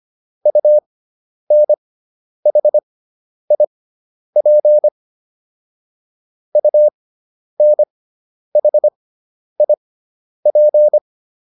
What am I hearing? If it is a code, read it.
Morse code, heard as a keyed tone, about 25 wpm, spaced as for 10 wpm, 600 Hz. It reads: UNHIP UNHIP